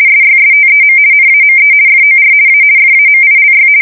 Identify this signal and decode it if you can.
RTTY, about 50 baud. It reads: SYSTEM NAME FOR A FAMILY
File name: fsk170-50B / teletype-corp